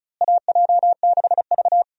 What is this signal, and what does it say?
Morse code, heard as a keyed tone, about 36 wpm, 700 Hz. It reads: AJ6V
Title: USA Call Signs - 4 Characters - Repeated 3x - 35wpm